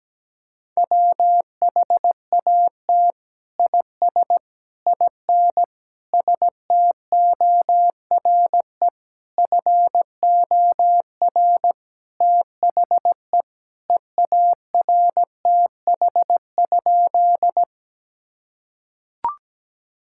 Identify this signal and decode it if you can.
Morse code, heard as a keyed tone, about 17 wpm, 700 Hz. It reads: WHAT IS IN STORE FOR THE EARTH?
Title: Sentences from Top 1000 Words - Review Part 5 - 17wpm